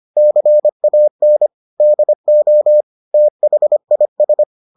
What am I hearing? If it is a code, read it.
Morse code, heard as a keyed tone, about 25 wpm, 600 Hz. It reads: CAN DO THIS